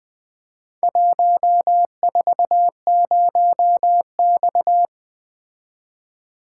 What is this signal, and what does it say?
Morse code, heard as a keyed tone, about 22 wpm, 700 Hz. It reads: 140X
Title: Speed-Racing - Four Letter-Number Combinations - 20wpm